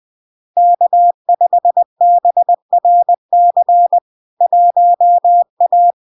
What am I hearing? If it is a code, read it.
Morse code, heard as a keyed tone, about 20 wpm, 700 Hz. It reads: K5BRC 1A
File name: LARK-FD-2022-Log-in-CW-20wpm-700-Hz